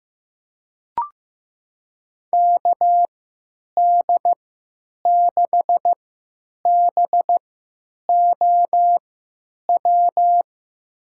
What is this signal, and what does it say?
Morse code, heard as a keyed tone, about 15 wpm, spaced as for 9 wpm, 700 Hz. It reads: KD6BOW
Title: USA Call Signs - 6 Characters - 3x Character Spacing - 15wpm